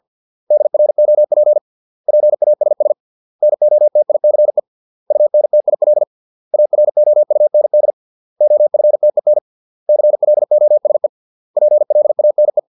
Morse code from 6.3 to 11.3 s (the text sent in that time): AROUND OFTEN CLOSE